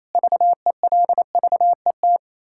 Morse code, heard as a keyed tone, about 28 wpm, 700 Hz. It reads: VELVET
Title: Pangram Challenge v2 - 28wpm